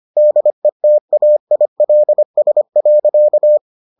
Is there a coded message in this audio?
Yes: Morse code from 0.2 to 3.6 s, heard as a keyed tone, about 25 wpm, 600 Hz. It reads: DETAILS.